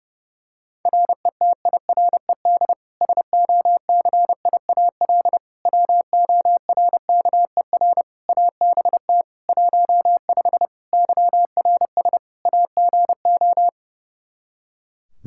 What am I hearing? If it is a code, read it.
Morse code, heard as a keyed tone, about 30 wpm, 700 Hz. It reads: RETIRED SOCIAL WORKER ABT 15 YRS AGO
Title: Practice QSO - Exchange 3 - More Info - 30wpm